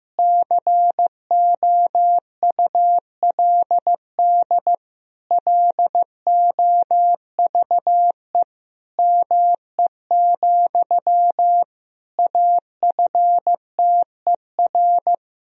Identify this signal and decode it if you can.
Morse code, heard as a keyed tone, about 15 wpm, 700 Hz. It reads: COULD LOVE ME, AFTER
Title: Sentences from Top 1000 Words - Review Part 1 - 15wpm